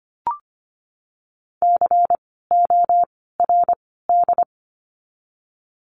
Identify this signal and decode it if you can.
Morse code, heard as a keyed tone, about 25 wpm, 700 Hz. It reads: C O R D
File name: course / 25:10wpm / 094 Lesson 16f - Letter c - Words 25wpm